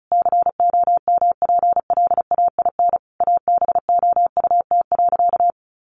Morse code, heard as a keyed tone, about 34 wpm, 700 Hz. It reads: COMPLAIN ABOUT.
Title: Sentences from Top 1300 Words 35wpm